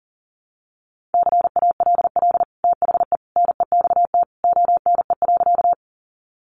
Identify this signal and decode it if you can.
Morse code, heard as a keyed tone, about 40 wpm, 700 Hz. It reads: CALL THE NEXT ONE.
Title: Sentences from Top 300 Words 40wpm